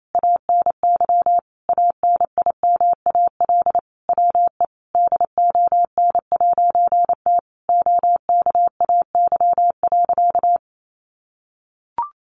Morse code, heard as a keyed tone, about 28 wpm, 700 Hz. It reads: ANY ANIMAL WE DON'T OKAY.
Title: A Sound Of Thunder 28wpm